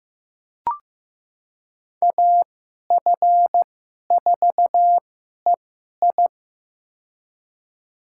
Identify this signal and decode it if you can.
Morse code, heard as a keyed tone, about 15 wpm, spaced as for 11 wpm, 700 Hz. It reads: AF4EI
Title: USA Call Signs - 5 Characters - 2x Character Spacing - 15wpm